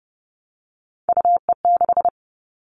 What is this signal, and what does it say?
Morse code, heard as a keyed tone, about 30 wpm, 700 Hz. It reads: UE6
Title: All Cut Numbers - 3-Digit 30wpm